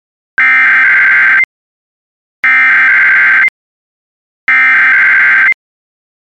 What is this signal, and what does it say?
Emergency Alert System (SAME) tones (header bursts), heard as harsh buzzing bursts. It is heard from 0.4 to 5.5 s.